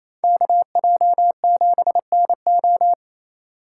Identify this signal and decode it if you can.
Morse code, heard as a keyed tone, about 34 wpm, 700 Hz. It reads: KJ7NO
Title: Speed Racing - USA 2x2 Call Signs 28wpm